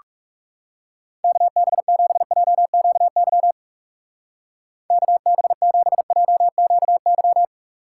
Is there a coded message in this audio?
Yes: Morse code from 1.2 to 7.5 s, heard as a keyed tone, about 38 wpm, 700 Hz. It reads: KB7JQY KB7JQY